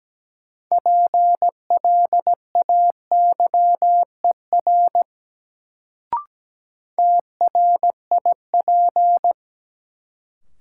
Morse code, heard as a keyed tone, about 17 wpm, 700 Hz. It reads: PLAYER TRIP
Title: Sentences from Top 1000 Words - Review Part 4 - New Word Drill - 17wpm